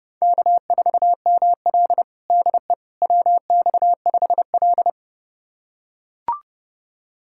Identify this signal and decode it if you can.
Morse code, heard as a keyed tone, about 30 wpm, 700 Hz. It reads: K4ML DE WX5L
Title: USA Callsign DE Callsign - 4 Characters - 30wpm